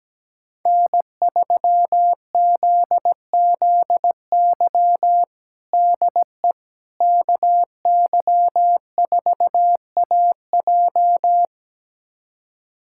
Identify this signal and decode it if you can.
Morse code, heard as a keyed tone, about 17 wpm, 700 Hz. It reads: N3ZZY DE KY4AJ
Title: USA Callsign DE Callsign - 5 Characters - 17wpm